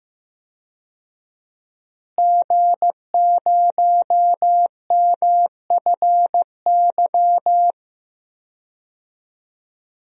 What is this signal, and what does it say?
Morse code, heard as a keyed tone, about 15 wpm, 700 Hz. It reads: G0MFY
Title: DXCC-100 Call Signs - 5 Characters - Repeated 3x - 15wpm